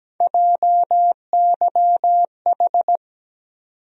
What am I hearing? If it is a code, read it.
Morse code, heard as a keyed tone, about 17 wpm, 700 Hz. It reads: JYH
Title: USA Call Sign Postfixes - 3 Letters 17wpm